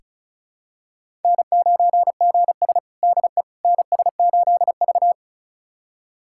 Morse code, heard as a keyed tone, about 36 wpm, 700 Hz. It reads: N9GS DE NS8V
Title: USA Callsign DE Callsign - 4 Characters - 35wpm